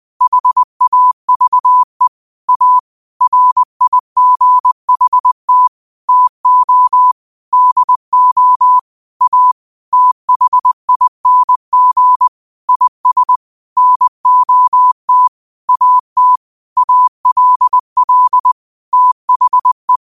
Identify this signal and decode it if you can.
Morse code, heard as a keyed tone, about 20 wpm, 1000 Hz. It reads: HAVE A RIGHT TO DO A THING IS NOT AT ALL THE
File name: qotdfiles / 20wpm